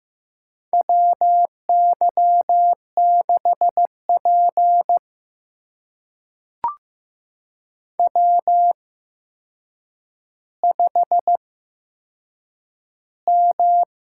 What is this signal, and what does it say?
Morse code, heard as a keyed tone, about 15 wpm, spaced as for 7 wpm, 700 Hz. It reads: WY6P W5M
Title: USA Call Signs - 8x Character Spacing - 15wpm